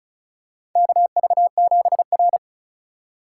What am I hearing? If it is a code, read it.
Morse code, heard as a keyed tone, about 36 wpm, 700 Hz. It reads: KV7R
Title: USA Call Signs - 4 Characters - 4x Character Spacing - 35wpm